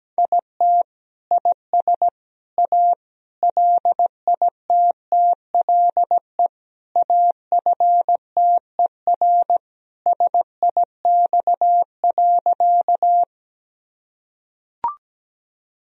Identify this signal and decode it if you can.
Morse code, heard as a keyed tone, about 17 wpm, 700 Hz. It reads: IT IS A LITTLE AFTER SIX.
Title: Sentences from Top 1000 Words - Review Part 2 - 17wpm